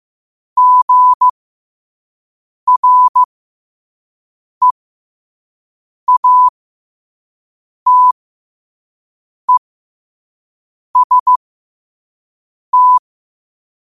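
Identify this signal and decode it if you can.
Morse code, heard as a keyed tone, about 15 wpm, spaced as for 5 wpm, 1000 Hz. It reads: GREATEST
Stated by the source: Morse at 5 wpm.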